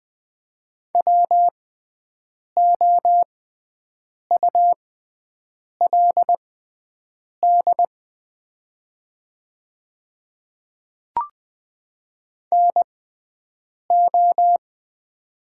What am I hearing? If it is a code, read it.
Morse code, heard as a keyed tone, about 20 wpm, spaced as for 7 wpm, 700 Hz. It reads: WOULD NO
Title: 6x Character Spacing - Top 100 Words - 20wpm